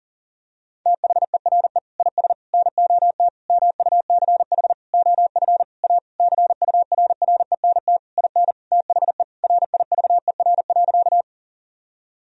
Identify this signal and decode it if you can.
Morse code, heard as a keyed tone, about 37 wpm, 700 Hz. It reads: THERE IS NOT MUCH OF A CURRENT IN THE RIVER.
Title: Sentences from Top 1000 Words - Review Part 4 - 40wpm